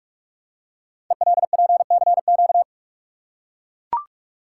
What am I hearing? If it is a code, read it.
Morse code, heard as a keyed tone, about 44 wpm, 690 Hz. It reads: ELPCQ
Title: Code Groups - 5 Letters - 45wpm